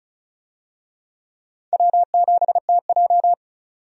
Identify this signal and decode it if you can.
Morse code, heard as a keyed tone, about 34 wpm, 700 Hz. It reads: W7TJ
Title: USA Call Signs - 4 Characters - 35wpm